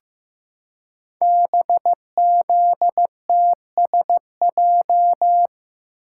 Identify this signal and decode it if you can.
Morse code, heard as a keyed tone, about 15 wpm, 700 Hz. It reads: BZTSJ
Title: Code Groups - 5 Letters - 15wpm